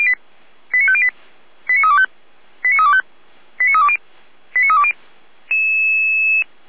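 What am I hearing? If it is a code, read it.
Five-tone selective call (ZVEI), heard as quick runs of five tones, tones of about 70 ms. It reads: 89508 89325 89325 89320 89320
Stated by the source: ZVEI five-tone call 89329 received on an SDR.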